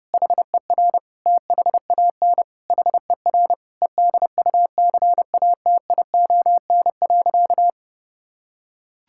Morse code, heard as a keyed tone, about 30 wpm, 700 Hz. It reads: HER THAN HER EDUCATION.
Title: Sentences from Top 1000 Words - Review Part 6 - 30wpm